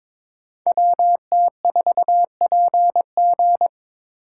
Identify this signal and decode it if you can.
Morse code, heard as a keyed tone, about 26 wpm, 700 Hz. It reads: WT4PG
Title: Speed Racing - USA 2x2 Call Signs 22wpm